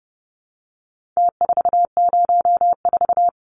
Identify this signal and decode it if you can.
Morse code, heard as a keyed tone, about 30 wpm, 700 Hz. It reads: T404
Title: Numbers - Contest Style - 4-Digit 30wpm